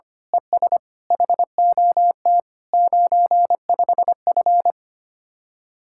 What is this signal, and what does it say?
Morse code, heard as a keyed tone, about 25 wpm, 700 Hz. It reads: ES HOT 95F
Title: Speed-Racing Common QSO Phrases - 25wpm